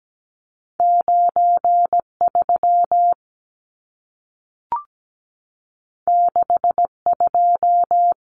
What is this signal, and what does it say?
Morse code, heard as a keyed tone, about 17 wpm, 700 Hz. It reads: 93 62
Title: Numbers - 2-Digit 17wpm